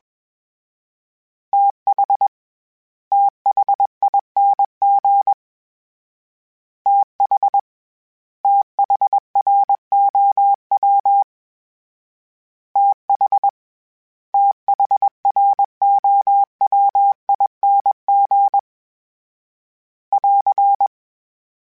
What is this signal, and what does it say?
Morse code, heard as a keyed tone, about 21 wpm, 800 Hz. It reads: TH THING TH THROW TH THROWING <AR>